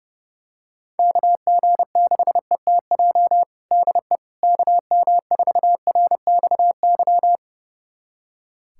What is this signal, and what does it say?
Morse code, heard as a keyed tone, about 32 wpm, 700 Hz. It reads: KG6ETJ DE KM4RXY
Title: USA Callsign DE Callsign 30wpm